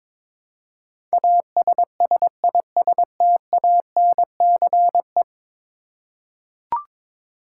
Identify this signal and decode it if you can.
Morse code, heard as a keyed tone, about 22 wpm, 700 Hz. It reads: ASSISTANCE
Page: morse-code-ninja-practice-22wpm